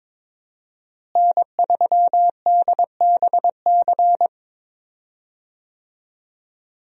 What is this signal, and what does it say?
Morse code, heard as a keyed tone, about 22 wpm, 700 Hz. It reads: N3DBC
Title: USA Call Signs - 5 Characters - Repeated 2x - 22wpm